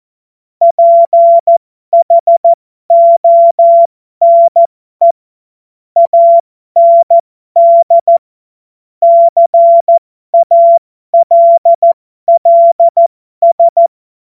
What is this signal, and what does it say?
Morse code, heard as a keyed tone, about 14 wpm, spaced as for 12 wpm, 680 Hz. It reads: PHONE AND CALLS
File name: Morsekurs / Witze